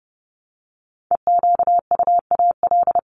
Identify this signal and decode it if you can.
Morse code, heard as a keyed tone, about 30 wpm, 700 Hz. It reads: EQUAL
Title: LY Suffix - Top 100 Words with Base Words 30wpm